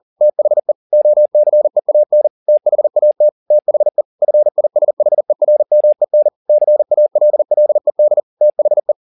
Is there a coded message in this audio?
Yes: Morse code from 0.2 to 8.9 s, heard as a keyed tone, about 40 wpm, 600 Hz. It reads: THE OCEAN THAT THE FISHERMEN CALLED THE